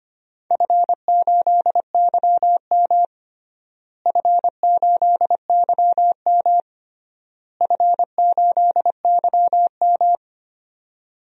Morse code, heard as a keyed tone, about 25 wpm, 700 Hz. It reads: F8YM F8YM F8YM
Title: DXCC-100 Call Signs - 4 Characters - Repeated 3x - 25wpm